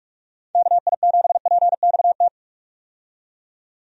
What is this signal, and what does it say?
Morse code, heard as a keyed tone, about 46 wpm, 700 Hz. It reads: KI7PXT
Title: USA Call Signs - 6 Characters - 45wpm